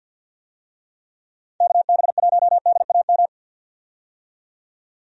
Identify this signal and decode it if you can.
Morse code, heard as a keyed tone, about 47 wpm, 700 Hz. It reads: KB1DAM
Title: USA Call Signs - 6 Characters - 50wpm